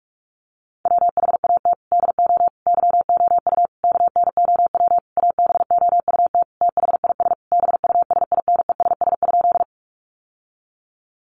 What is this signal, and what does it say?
Morse code, heard as a keyed tone, about 44 wpm, 700 Hz. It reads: WHAT DO YOU KNOW ABOUT THIS BUSINESS?
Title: Sentences from Top 600 Words 45wpm